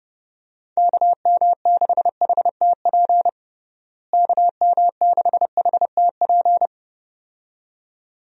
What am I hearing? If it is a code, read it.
Morse code, heard as a keyed tone, about 31 wpm, 700 Hz. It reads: KM6HTP KM6HTP
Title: USA Call Signs - 6 Characters - Repeated 2x - 30wpm